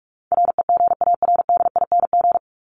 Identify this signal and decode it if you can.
Morse code, heard as a keyed tone, about 44 wpm, 700 Hz. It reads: REGARDING
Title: ING Suffix - Top 100 Words with Base Words 45wpm